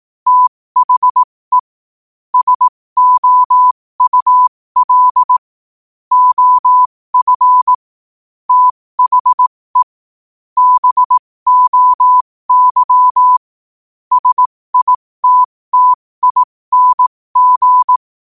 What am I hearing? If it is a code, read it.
Morse code, heard as a keyed tone, about 18 wpm, spaced as for 15 wpm, 1000 Hz. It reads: THE SOUL OF THE BOY SITTING
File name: qotdfiles / 15wpm